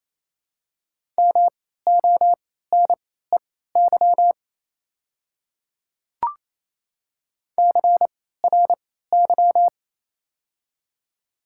Morse code, heard as a keyed tone, about 28 wpm, spaced as for 16 wpm, 700 Hz. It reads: MONEY CRY
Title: 3x Character Spacing - Top 500 Words - 28wpm